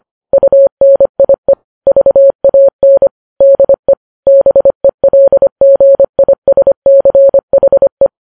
Morse code, heard as a keyed tone, about 25 wpm, 550 Hz. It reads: UNIE VAN DE BELGISCHE
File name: Files Web tls / Morse on9fv / morse25kps